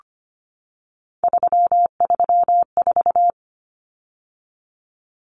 Morse code, heard as a keyed tone, about 25 wpm, 700 Hz. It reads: 334
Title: Numbers - 3-Digit 25wpm